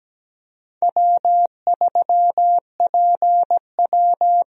Morse code, heard as a keyed tone, about 17 wpm, 700 Hz. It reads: W3PW
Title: USA Call Signs - 4 Characters - Repeated 3x - 17wpm